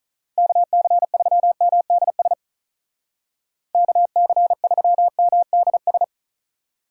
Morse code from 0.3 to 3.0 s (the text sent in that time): KC3MDS